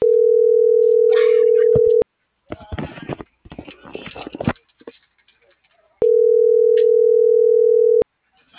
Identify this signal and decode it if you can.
Ringback tone (North America), 440 + 480 Hz.